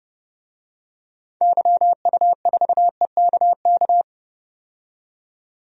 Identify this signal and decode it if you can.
Morse code, heard as a keyed tone, about 30 wpm, 700 Hz. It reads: YU4EKK
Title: DXCC-100 Call Signs - 6 Characters - 30wpm